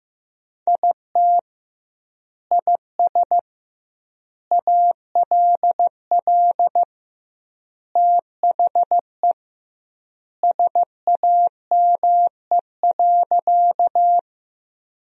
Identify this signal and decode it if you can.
Morse code, heard as a keyed tone, about 15 wpm, 700 Hz. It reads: IT IS ALL THE SAME.
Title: 2x Word-Spacing - Sentences from Top 200 Words - 15wpm